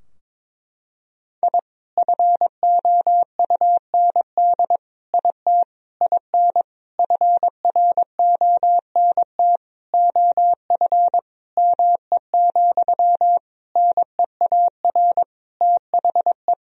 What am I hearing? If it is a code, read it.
Morse code, heard as a keyed tone, about 22 wpm, 700 Hz. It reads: I FOUND IT IN FRONT OF ME, NEAR THE
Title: Sentences from Top 1000 Words - Review Part 2 - 22wpm